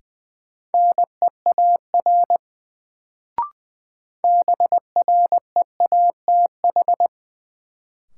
Morse code, heard as a keyed tone, about 20 wpm, 700 Hz. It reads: NEAR BREATH
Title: Sets of 5 Words Encore - New Word Drill - 20wpm